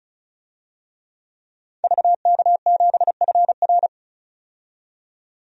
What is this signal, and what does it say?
Morse code, heard as a keyed tone, about 36 wpm, 700 Hz. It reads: VK7FR
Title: DXCC-100 Call Signs - 5 Characters - 35wpm